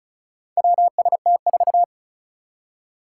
Morse code, heard as a keyed tone, about 36 wpm, 700 Hz. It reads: WST4
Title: Four Challenging Letter-Number Combinations 35wpm